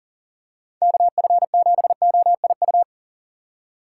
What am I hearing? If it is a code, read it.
Morse code, heard as a keyed tone, about 40 wpm, 700 Hz. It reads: KF7OIU
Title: USA Call Signs - 6 Characters - 2x Character Spacing - 40wpm